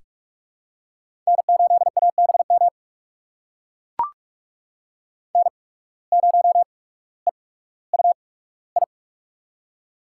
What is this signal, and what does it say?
Morse code, heard as a keyed tone, about 44 wpm, spaced as for 19 wpm, 690 Hz. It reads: N8ABM N0EUI
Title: USA Call Signs - 5 Characters - 8x Character Spacing - 45wpm